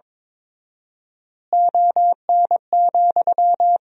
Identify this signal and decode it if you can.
Morse code, heard as a keyed tone, about 23 wpm, 700 Hz. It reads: ON,
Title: 4x Word-Spacing - Sentences from Top 500 Words - 22wpm